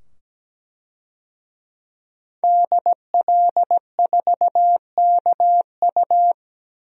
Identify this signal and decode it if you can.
Morse code, heard as a keyed tone, about 17 wpm, 700 Hz. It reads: DL4KU
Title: DXCC-100 Call Signs - 5 Characters - 17wpm